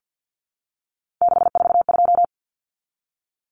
Morse code, heard as a keyed tone, about 50 wpm, 700 Hz. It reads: 642
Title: Numbers - 3-Digit 50wpm